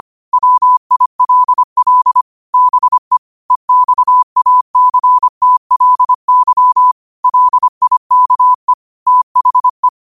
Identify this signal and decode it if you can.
Morse code, heard as a keyed tone, about 25 wpm, 1000 Hz. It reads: WILL BE EXACTLY LIKE THE
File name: qotdfiles / 25wpm